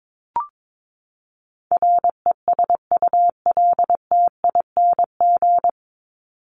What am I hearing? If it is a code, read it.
Morse code, heard as a keyed tone, about 22 wpm, 700 Hz. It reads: RESULTING
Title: ING Suffix - Top 100 Words with Base Words 22wpm